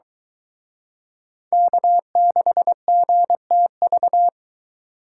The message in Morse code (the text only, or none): K6GTV